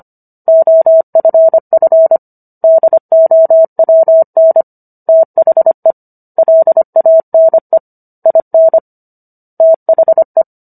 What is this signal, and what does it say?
Morse code, heard as a keyed tone, about 25 wpm, 650 Hz. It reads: OFF DOWN THE LANE IN THE